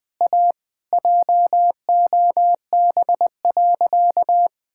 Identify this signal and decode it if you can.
Morse code, heard as a keyed tone, about 21 wpm, 700 Hz. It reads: A JOB.
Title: Sentences with 3-Word Phrases - 20wpm